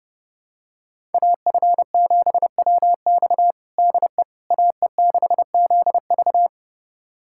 Morse code, heard as a keyed tone, about 32 wpm, 700 Hz. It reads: AF7WX DE AE6ZV
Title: USA Callsign DE Callsign - 5 Characters - 30wpm